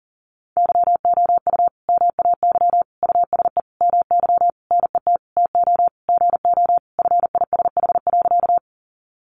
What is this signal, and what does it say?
Morse code, heard as a keyed tone, about 40 wpm, 700 Hz. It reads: YOU MAY USE MY NET TO GO FISH.
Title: Sentences from Top 1400 Words 40wpm